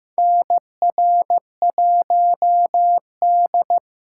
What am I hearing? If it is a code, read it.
Morse code, heard as a keyed tone, about 15 wpm, 700 Hz. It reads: NR1D
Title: USA Call Signs - 4 Characters - 6x Character Spacing - 15wpm